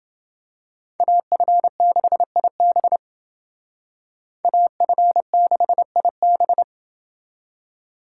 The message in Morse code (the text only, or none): AF6IB AF6IB